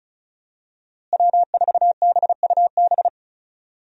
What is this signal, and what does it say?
Morse code, heard as a keyed tone, about 36 wpm, 700 Hz. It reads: W4BUB
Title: USA Call Signs - 5 Characters - Repeated 3x - 35wpm